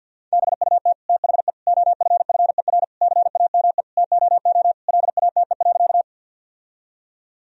Morse code, heard as a keyed tone, about 50 wpm, 700 Hz. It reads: BUT THE OFFER CAME TOO LATE.
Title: Sentences from Top 1000 Words - Review Part 4 - 50wpm